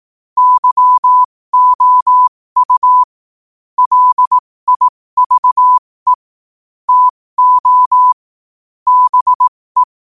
Morse code, heard as a keyed tone, about 18 wpm, spaced as for 15 wpm, 1000 Hz. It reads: YOU LIVE TO BE
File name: qotdfiles / 15wpm